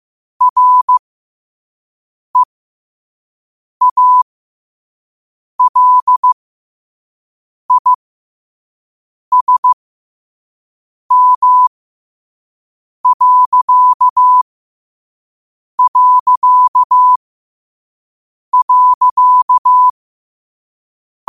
Morse code, heard as a keyed tone, about 15 wpm, spaced as for 5 wpm, 1000 Hz. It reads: REALISM...
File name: 5wpm